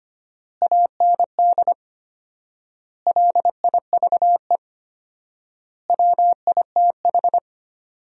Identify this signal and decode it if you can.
Morse code, heard as a keyed tone, about 25 wpm, 700 Hz. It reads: AND LIVE WITH
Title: 4x Word-Spacing - Sentences from Top 500 Words - 25wpm